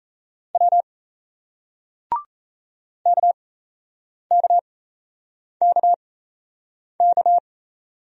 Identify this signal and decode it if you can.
Morse code, heard as a keyed tone, about 41 wpm, spaced as for 8 wpm, 700 Hz. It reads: W KKKK